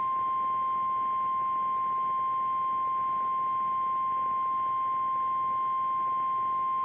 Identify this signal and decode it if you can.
Emergency Alert System (SAME) tones (the 1050 Hz attention tone). It is heard from 1.0 to 6.0 s.
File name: eas / KKPRKXPN / KKPRKXPN